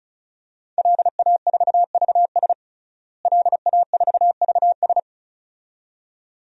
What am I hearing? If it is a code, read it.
Morse code, heard as a keyed tone, about 36 wpm, 700 Hz. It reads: LA4VS LA4VS
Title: DXCC-100 Call Signs - 5 Characters - Repeated 2x - 35wpm